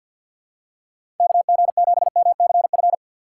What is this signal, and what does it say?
Morse code, heard as a keyed tone, about 47 wpm, 700 Hz. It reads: KG7MCF
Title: USA Call Signs - 6x Character Spacing - 50wpm